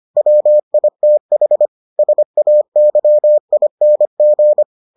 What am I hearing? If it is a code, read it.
Morse code, heard as a keyed tone, about 25 wpm, 600 Hz. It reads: WITH SAYING